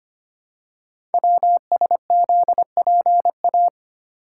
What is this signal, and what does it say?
Morse code, heard as a keyed tone, about 25 wpm, 700 Hz. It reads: WSZPA
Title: Code Groups - 5 Letters - 25wpm